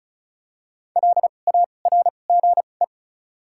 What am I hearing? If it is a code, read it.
Morse code, heard as a keyed tone, about 34 wpm, spaced as for 25 wpm, 700 Hz. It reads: LARGE
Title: 2x Character Spacing - Top 200 Words - 35wpm